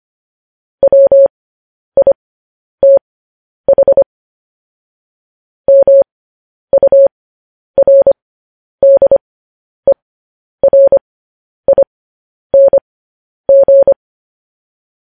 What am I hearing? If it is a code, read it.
Morse code, heard as a keyed tone, about 25 wpm, spaced as for 10 wpm, 550 Hz. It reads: WITH MURDERING
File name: podcast / News headlines-1x-25x10